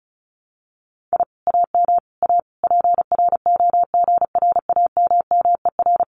Morse code, heard as a keyed tone, about 34 wpm, 700 Hz. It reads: I AM A PROGRAMMER
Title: Common QSO Phrases Encore 35wpm